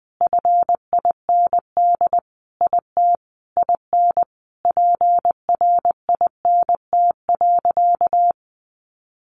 Morse code, heard as a keyed tone, about 20 wpm, 700 Hz. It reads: FIND IT IN PRINT.
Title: Sentences from Top 1400 Words 20wpm